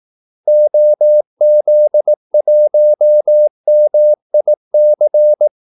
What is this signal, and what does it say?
Morse code, heard as a keyed tone, about 18 wpm, 600 Hz. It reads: OZ1MIC
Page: Kaldesignaler 18 wpm